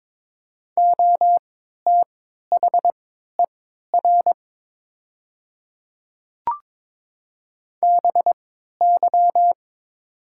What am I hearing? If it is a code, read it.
Morse code, heard as a keyed tone, about 22 wpm, spaced as for 12 wpm, 700 Hz. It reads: OTHER BY
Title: 3x Character Spacing - Top 100 Words - 22wpm